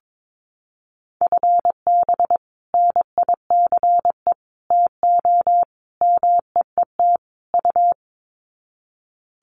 Morse code, heard as a keyed tone, about 22 wpm, 700 Hz. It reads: FB NICE TO MEET U
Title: Common QSO Phrases 22wpm